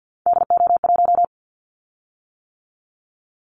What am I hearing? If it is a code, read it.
Morse code, heard as a keyed tone, about 50 wpm, 700 Hz. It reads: DO1
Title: Callsign Prefixes for Top 50 DXCC - 50wpm